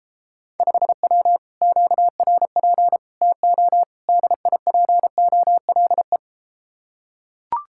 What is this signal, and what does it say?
Morse code, heard as a keyed tone, about 33 wpm, 700 Hz. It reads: 5W QRP TO DIPOLE